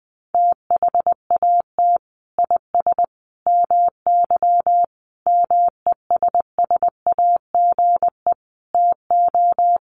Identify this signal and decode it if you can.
Morse code, heard as a keyed tone, about 20 wpm, 700 Hz. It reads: THAT IS MY MESSAGE TO
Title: Sentences from Top 1100 Words 20wpm